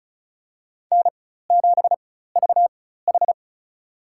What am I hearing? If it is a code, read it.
Morse code, heard as a keyed tone, about 36 wpm, spaced as for 16 wpm, 700 Hz. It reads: N7VH
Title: USA Call Signs - 4 Characters - 4x Character Spacing - 35wpm